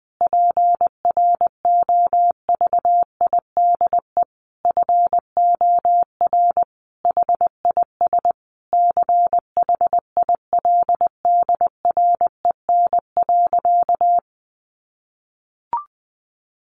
Morse code, heard as a keyed tone, about 20 wpm, 700 Hz. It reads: PROVIDE FOR HIS CHILDREN.
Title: Sentences from Top 1100 Words 20wpm